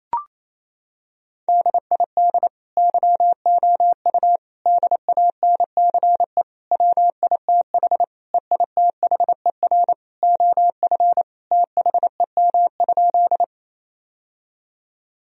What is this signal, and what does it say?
Morse code, heard as a keyed tone, about 28 wpm, 700 Hz. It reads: DID YOU DANCE WITH EITHER OF THEM?
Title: Sentences from Top 1000 Words - Review Part 4 - 28wpm